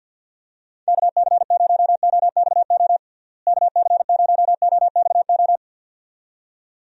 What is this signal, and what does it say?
Morse code, heard as a keyed tone, about 50 wpm, 700 Hz. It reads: KC0OXO KC0OXO
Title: USA Call Signs - 6 Characters - Repeated 2x - 50wpm